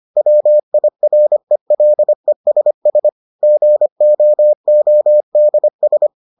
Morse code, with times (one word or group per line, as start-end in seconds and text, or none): 0.2-3.1 WIRELESS
3.4-6.1 GOODS